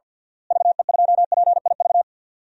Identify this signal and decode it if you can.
Morse code, heard as a keyed tone, about 53 wpm, 700 Hz. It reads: VE2PIV
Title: DXCC-100 Call Signs - 6 Characters - Repeated 3x - 50wpm